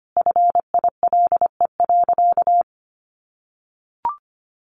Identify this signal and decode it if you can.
Morse code, heard as a keyed tone, about 25 wpm, 700 Hz. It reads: FILE.
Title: Sentences from Top 1200 Words 25wpm